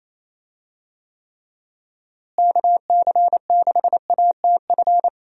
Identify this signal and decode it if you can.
Morse code, heard as a keyed tone, about 27 wpm, 700 Hz. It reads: KC6ATF
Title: USA Call Signs - 6 Characters - Repeated 3x - 28wpm